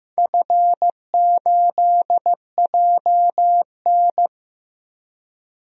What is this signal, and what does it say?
Morse code, heard as a keyed tone, about 15 wpm, 700 Hz. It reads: F8JN
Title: DXCC-100 Call Signs - 4 Characters - Repeated 3x - 15wpm